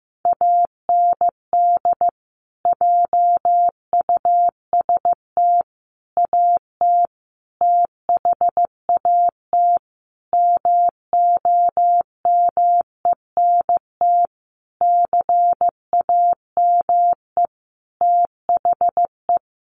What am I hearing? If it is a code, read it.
Morse code, heard as a keyed tone, about 15 wpm, 700 Hz. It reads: AND JUST AT THAT MOMENT CAME THE
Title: Sentences from Top 1500 Words 15wpm